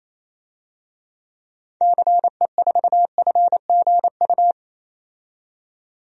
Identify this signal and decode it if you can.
Morse code, heard as a keyed tone, about 28 wpm, 700 Hz. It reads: CE4FGU
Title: DXCC-100 Call Signs - 6 Characters - 28wpm